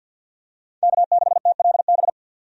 Morse code, heard as a keyed tone, about 48 wpm, 700 Hz. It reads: K6TLB
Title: USA Call Signs - 5 Characters - 50wpm